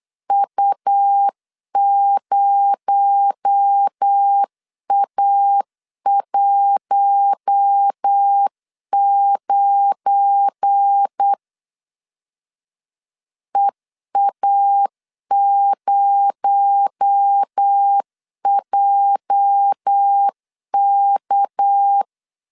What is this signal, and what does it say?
Morse code, heard as a keyed tone, about 8 wpm, 800 Hz. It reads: U0A19 EA0JK
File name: CW / DET / 1 9 1500C S A 8wpm 5637